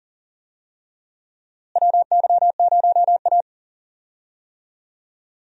Morse code, heard as a keyed tone, about 40 wpm, 700 Hz. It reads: WY0A
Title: USA Call Signs - 4 Characters - Repeated 2x - 40wpm